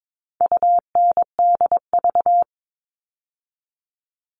Morse code, heard as a keyed tone, about 22 wpm, 700 Hz. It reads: UNDV
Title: All Cut Numbers - 4-Digit 22wpm